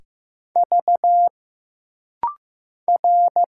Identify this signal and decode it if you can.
Morse code, heard as a keyed tone, about 15 wpm, 700 Hz. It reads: V R
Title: Single Letters - Rapid-Fire 15wpm